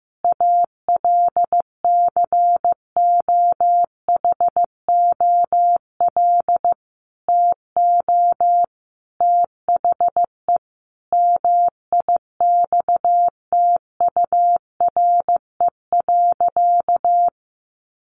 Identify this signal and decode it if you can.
Morse code, heard as a keyed tone, about 15 wpm, 700 Hz. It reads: ALCOHOL TO THE MIXTURE.
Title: Sentences from Top 1500 Words 15wpm